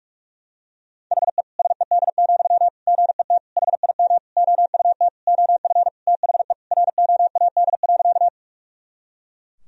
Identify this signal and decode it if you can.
Morse code, heard as a keyed tone, about 41 wpm, 700 Hz. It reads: HE SED, GET HIM OUT OF THE ROAD.
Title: Sentences from Top 400 Words and Abbreviations 45wpm